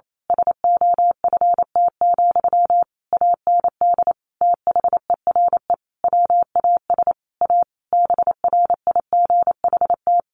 Morse code, heard as a keyed tone, about 28 wpm, 700 Hz. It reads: SOFT, AND THERE WAS A BRIGHT
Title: Sentences from Top 1500 Words 28wpm